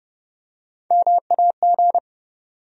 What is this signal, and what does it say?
Morse code, heard as a keyed tone, about 30 wpm, 700 Hz. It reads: MAG
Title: USA Call Sign Postfixes - 3 Letters 30wpm